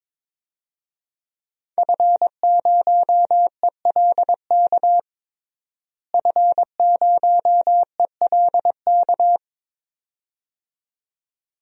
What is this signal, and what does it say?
Morse code, heard as a keyed tone, about 22 wpm, 700 Hz. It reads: F0ELK F0ELK
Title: DXCC-100 Call Signs - 5 Characters - Repeated 2x - 22wpm